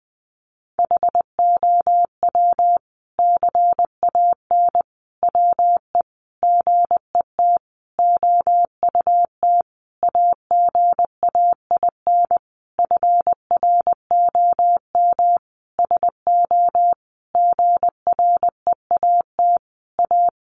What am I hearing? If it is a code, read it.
Morse code, heard as a keyed tone, about 20 wpm, 700 Hz. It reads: HOW CAN WE GET OUT AGAIN FROM SO GREAT A
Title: Sentences from Top 1200 Words 20wpm